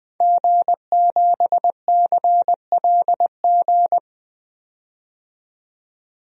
Morse code, heard as a keyed tone, about 20 wpm, 700 Hz. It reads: G7CLG
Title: DXCC-100 Call Signs - 5 Characters - 20wpm